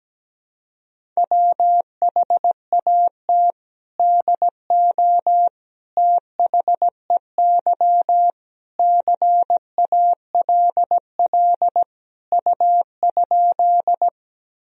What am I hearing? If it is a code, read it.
Morse code, heard as a keyed tone, about 17 wpm, 700 Hz. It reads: WHAT DO THEY CALL U?
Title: Sentences from Top 100 Words and Abbreviations 17wpm